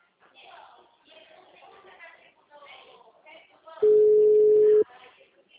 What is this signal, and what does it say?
Ringback tone (Europe, 425 Hz), with speech.